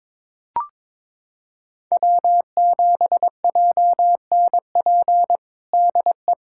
Morse code, heard as a keyed tone, about 21 wpm, 700 Hz. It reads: W7JNP DE
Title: USA Callsign DE Callsign 22wpm